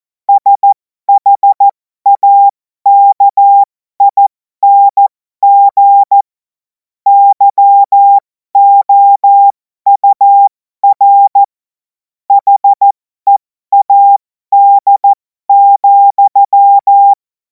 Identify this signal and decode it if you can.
Morse code, heard as a keyed tone, about 14 wpm, spaced as for 12 wpm, 800 Hz. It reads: SHAKING YOUR HEAD,
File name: Morsekurs / Witze